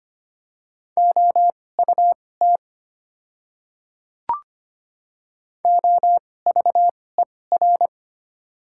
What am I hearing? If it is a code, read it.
Morse code, heard as a keyed tone, about 25 wpm, spaced as for 18 wpm, 690 Hz. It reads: OUT OVER
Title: 2x Character Spacing - Top 100 Words - 25wpm